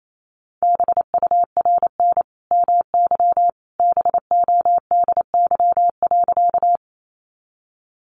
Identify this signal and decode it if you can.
Morse code, heard as a keyed tone, about 28 wpm, 700 Hz. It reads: BURN MY BODY.
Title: Sentences from Top 1300 Words 28wpm